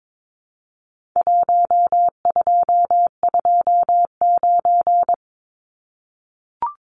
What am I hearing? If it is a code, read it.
Morse code, heard as a keyed tone, about 22 wpm, 700 Hz. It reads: 1229
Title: Numbers - Contest Style - 4-Digit 22wpm